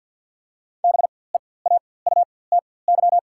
Morse code, heard as a keyed tone, about 48 wpm, spaced as for 23 wpm, 700 Hz. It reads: BEAUTY
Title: 4x Character Spacing - Top 500 Words - 50wpm